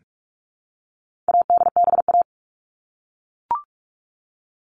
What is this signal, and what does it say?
Morse code, heard as a keyed tone, about 44 wpm, 700 Hz. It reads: ADBA
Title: All Cut Numbers - 4-Digit 45wpm